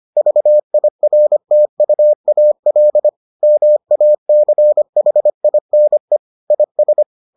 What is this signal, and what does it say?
Morse code, heard as a keyed tone, about 25 wpm, 600 Hz. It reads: VIRTUAL MACHINE IS